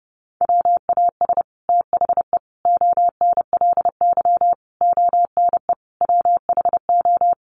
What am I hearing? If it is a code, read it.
Morse code, heard as a keyed tone, about 30 wpm, 700 Hz. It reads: WAS THE ONLY ONE WHO
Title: Sets of 5 Words Encore2 30wpm